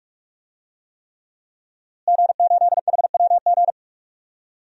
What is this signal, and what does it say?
Morse code, heard as a keyed tone, about 47 wpm, 700 Hz. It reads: G8HWG
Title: DXCC-100 Call Signs - 5 Characters - Repeated 3x - 45wpm